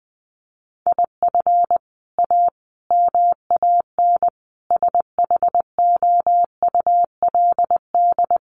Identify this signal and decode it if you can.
Morse code, heard as a keyed tone, about 20 wpm, 700 Hz. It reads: IF A MAN SHOULD
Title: Sentences from Top 1500 Words 20wpm